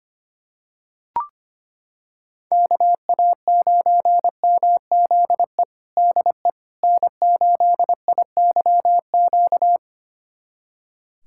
Morse code, heard as a keyed tone, about 25 wpm, 700 Hz. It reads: KA9MZE DE N8IYQ
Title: USA Callsign DE Callsign 25wpm